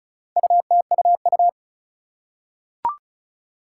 Morse code, heard as a keyed tone, about 36 wpm, 700 Hz. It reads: UTUU
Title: Four Challenging Letter-Number Combinations 35wpm